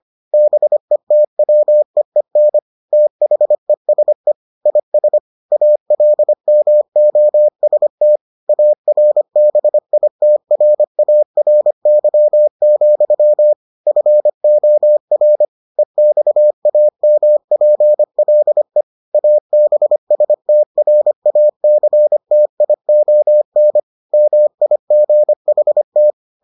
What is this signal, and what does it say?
Morse code, heard as a keyed tone, about 25 wpm, 600 Hz. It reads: BETWEEN THESE IS ALMOST ARBITRARY, FOR EXAMPLE ABSTRACTION MIGHT